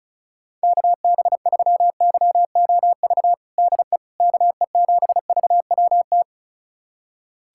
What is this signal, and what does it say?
Morse code, heard as a keyed tone, about 34 wpm, 700 Hz. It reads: KB3YOV DE KE7VWT
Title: USA Callsign DE Callsign - 6 Characters - 35wpm